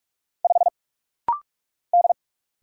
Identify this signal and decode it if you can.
Morse code, heard as a keyed tone, about 44 wpm, 690 Hz. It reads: 5 D
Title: Single Letter-Number - Rapid-Fire 45wpm